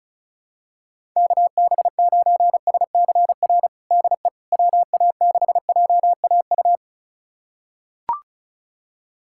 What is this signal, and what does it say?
Morse code, heard as a keyed tone, about 36 wpm, 700 Hz. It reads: KB9SCR DE WA6JAU
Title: USA Callsign DE Callsign - 6 Characters - 35wpm